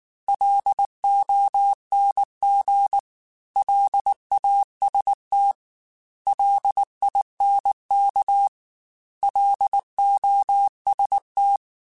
Morse code, heard as a keyed tone, about 19 wpm, 800 Hz. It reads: LONG LAST LINK LOST
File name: L Runs / L Runs mp3s / L Runs @18